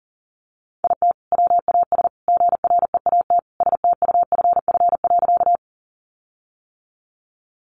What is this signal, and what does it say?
Morse code, heard as a keyed tone, about 40 wpm, 700 Hz. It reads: IT WAS GREAT STUFF.
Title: Sentences from Top 1400 Words 40wpm